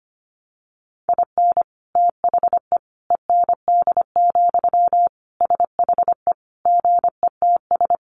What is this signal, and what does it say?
Morse code, heard as a keyed tone, about 25 wpm, 700 Hz. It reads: IN THE END, SHE GETS
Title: Additional Easy Phrases 25wpm